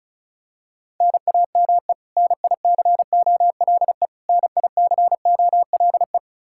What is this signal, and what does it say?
Morse code, heard as a keyed tone, about 36 wpm, 700 Hz. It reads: NAME NICOLE NICOLE
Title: Practice QSO - Exchange 1 - Beginning - 35wpm